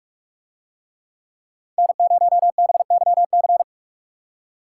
Morse code, heard as a keyed tone, about 40 wpm, 700 Hz. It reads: N0BYC